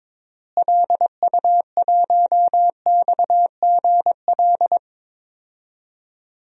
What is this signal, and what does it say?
Morse code, heard as a keyed tone, about 22 wpm, 700 Hz. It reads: LU1XGL
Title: DXCC-100 Call Signs - 6 Characters - Repeated 3x - 22wpm